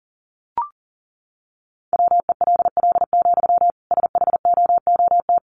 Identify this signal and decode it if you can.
Morse code, heard as a keyed tone, about 40 wpm, 700 Hz. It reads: WELL, SHOOT
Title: Sentences from Top 1200 Words 40wpm